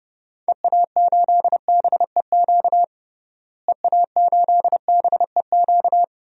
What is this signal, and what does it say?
Morse code, heard as a keyed tone, about 30 wpm, 700 Hz. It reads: EA8BEQ EA8BEQ
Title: DXCC-100 Call Signs - 6 Characters - Repeated 2x - 30wpm